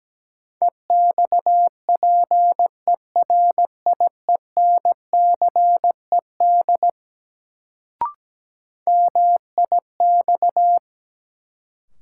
Morse code, heard as a keyed tone, about 17 wpm, 700 Hz. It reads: EXPERIENCED MIX
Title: Sentences from Top 1100 Words - New Word Drill Alt - 17wpm